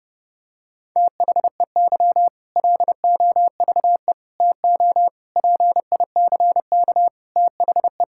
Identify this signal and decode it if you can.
Morse code, heard as a keyed tone, about 32 wpm, 700 Hz. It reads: THEY LOVE TO PICK THE
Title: Sentences from Top 1600 Words 30wpm